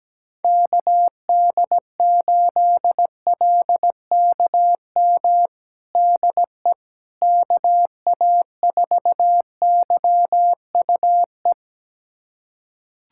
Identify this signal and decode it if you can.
Morse code, heard as a keyed tone, about 17 wpm, 700 Hz. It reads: KD8LKM DE KA4YUE
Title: USA Callsign DE Callsign - 6 Characters - 17wpm